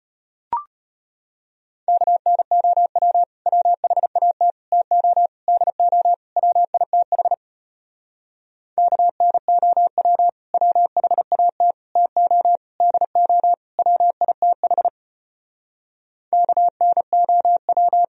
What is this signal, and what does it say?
Morse code, heard as a keyed tone, about 33 wpm, 700 Hz. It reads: KNOW WHAT TO DO WITH KNOW WHAT TO DO WITH KNOW